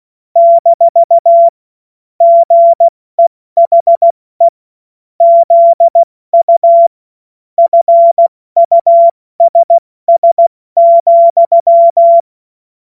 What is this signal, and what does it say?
Morse code, heard as a keyed tone, about 16 wpm, 680 Hz. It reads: - GEHE ZU FUSS,
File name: Witz-08-Juli-2025-70-F80